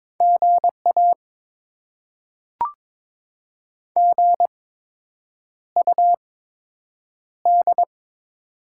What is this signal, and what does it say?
Morse code, heard as a keyed tone, about 22 wpm, spaced as for 8 wpm, 700 Hz. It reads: GA GUD
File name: Top 20 QSO Elements - 8x Character Spacing - 22wpm